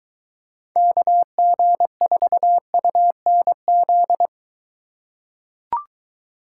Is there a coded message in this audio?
Yes: Morse code from 0.8 to 4.3 s, heard as a keyed tone, about 23 wpm, 700 Hz. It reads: KG4UNZ